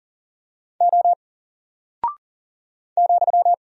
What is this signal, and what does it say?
Morse code, heard as a keyed tone, about 40 wpm, 700 Hz. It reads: O ,
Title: Single Character (Letter:Number:Punctuation:Prosign) 40wpm